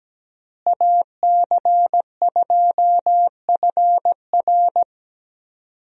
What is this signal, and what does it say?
Morse code, heard as a keyed tone, about 17 wpm, 700 Hz. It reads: AC2FR
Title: USA Call Signs - 5 Characters - Repeated 3x - 17wpm